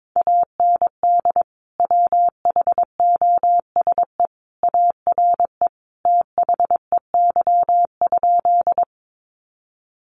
Morse code, heard as a keyed tone, about 22 wpm, 700 Hz. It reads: AND WHOSE ARE THEY?
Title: Sentences from Top 1200 Words 22wpm